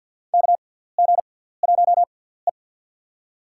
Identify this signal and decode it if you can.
Morse code, heard as a keyed tone, about 49 wpm, 700 Hz. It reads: K G 1 E
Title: USA Call Signs - 6x Character Spacing - 50wpm